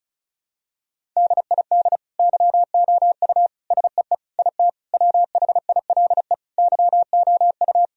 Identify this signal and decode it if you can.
Morse code, heard as a keyed tone, about 36 wpm, 700 Hz. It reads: DID YOU SEE IT WHILE YOU
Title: Sentences from Top 1000 Words - Review Part 1 - 35wpm